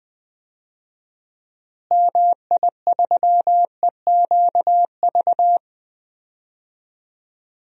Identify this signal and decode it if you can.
Morse code, heard as a keyed tone, about 20 wpm, 700 Hz. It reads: MI3EQV
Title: DXCC-100 Call Signs - 6 Characters - Repeated 3x - 20wpm